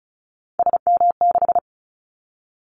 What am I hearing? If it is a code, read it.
Morse code, heard as a keyed tone, about 34 wpm, 700 Hz. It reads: SM6
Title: Callsign Prefixes for Top 25 DXCC - 35wpm